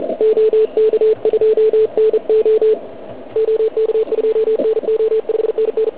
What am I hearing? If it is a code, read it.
Morse code, heard as a keyed tone, about 40 wpm, 440 Hz. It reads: OK2NO OK2NO 5NN